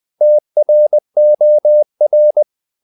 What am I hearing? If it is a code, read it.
Morse code, heard as a keyed tone, about 20 wpm, 600 Hz. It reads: TROR
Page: Ferie 20 wpm